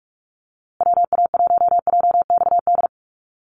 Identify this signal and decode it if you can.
Morse code, heard as a keyed tone, about 44 wpm, 700 Hz. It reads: WA1JXD
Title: USA Call Signs Repeated 2x 45wpm